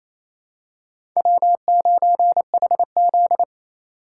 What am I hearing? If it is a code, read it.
Morse code, heard as a keyed tone, about 28 wpm, 700 Hz. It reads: W9HZ
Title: USA Call Signs - 4 Characters - 4x Character Spacing - 28wpm